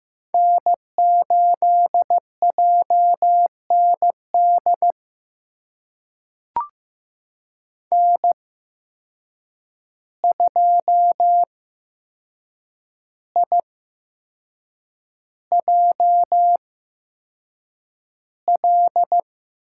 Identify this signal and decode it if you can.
Morse code, heard as a keyed tone, about 15 wpm, spaced as for 6 wpm, 700 Hz. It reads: N8JND N2IJL
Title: USA Call Signs - 5 Characters - 8x Character Spacing - 15wpm